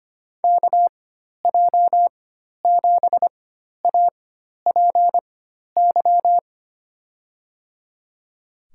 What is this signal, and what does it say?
Morse code, heard as a keyed tone, about 25 wpm, spaced as for 12 wpm, 700 Hz. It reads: KJ7APY
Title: USA Call Signs - 4x Character Spacing - 25wpm